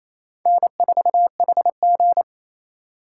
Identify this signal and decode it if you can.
Morse code, heard as a keyed tone, about 28 wpm, 700 Hz. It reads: N4HG
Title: USA Call Signs - 4 Characters - 3x Character Spacing - 28wpm